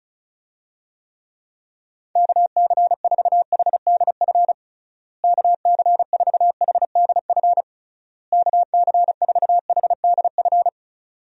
Morse code, heard as a keyed tone, about 36 wpm, 700 Hz. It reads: KC4HDF KC4HDF KC4HDF
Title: USA Call Signs - 6 Characters - Repeated 3x - 35wpm